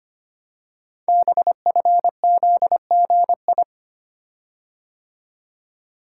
Morse code, heard as a keyed tone, about 25 wpm, 700 Hz. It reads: BFZGI
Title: Code Groups - 5 Letters - 25wpm